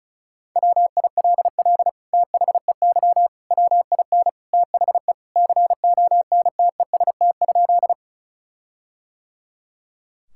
Morse code, heard as a keyed tone, about 36 wpm, 700 Hz. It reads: WILL THEY WIN THE CONTEST?
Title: Sentences from Top 1600 Words 35wpm